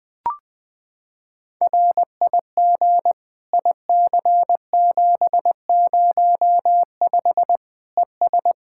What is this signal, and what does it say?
Morse code, heard as a keyed tone, about 20 wpm, 700 Hz. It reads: RIG IC705 ES
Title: Practice QSO - Exchange 2 - Info - 20wpm